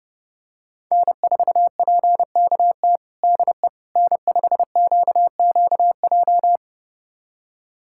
Morse code, heard as a keyed tone, about 32 wpm, 700 Hz. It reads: N4PKT DE N5QQJ
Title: USA Callsign DE Callsign - 5 Characters - 30wpm